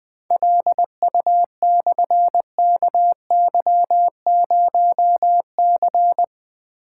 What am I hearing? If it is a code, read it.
Morse code, heard as a keyed tone, about 20 wpm, 700 Hz. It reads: LU/KY0C
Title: Difficult Call Signs 20wpm